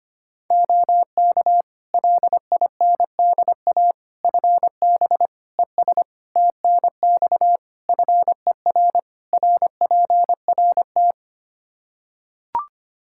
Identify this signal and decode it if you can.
Morse code, heard as a keyed tone, about 25 wpm, 700 Hz. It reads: OK LINDA FB ES TNX FER RPRT
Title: Practice QSO - Exchange 2 - Info - 25wpm